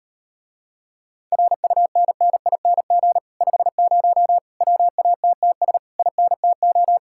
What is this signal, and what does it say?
Morse code, heard as a keyed tone, about 38 wpm, 700 Hz. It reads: RUNNING 50 WATTS INTO